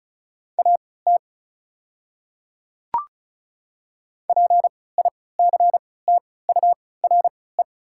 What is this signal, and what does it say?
Morse code, heard as a keyed tone, about 36 wpm, spaced as for 20 wpm, 700 Hz. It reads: AT PICTURE
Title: 3x Character Spacing - Top 200 Words - 35wpm